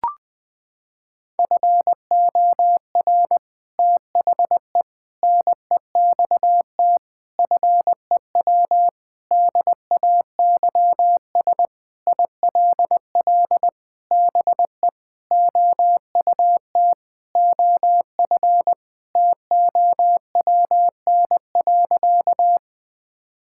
Morse code, heard as a keyed tone, about 20 wpm, 700 Hz. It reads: FOR THE NEXT FEW DAYS ILL BE OUT OF TOWN.
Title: Sentences with 3-Word Phrases - 20wpm